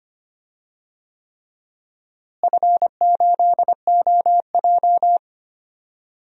Morse code, heard as a keyed tone, about 25 wpm, 700 Hz. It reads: F8OJ
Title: DXCC-100 Call Signs - 4 Characters - Repeated 3x - 25wpm